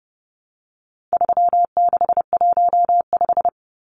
Morse code, heard as a keyed tone, about 30 wpm, 700 Hz. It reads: 3615
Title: Numbers - 4-Digit 30wpm